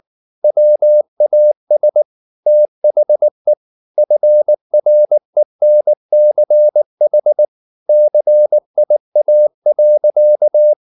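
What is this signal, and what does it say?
Morse code, heard as a keyed tone, about 19 wpm, 600 Hz. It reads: WAS THE FRENCH CIA.
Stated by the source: Morse code at 19 wpm from CW med Gnister, reading Ken Follett's novel Never in English.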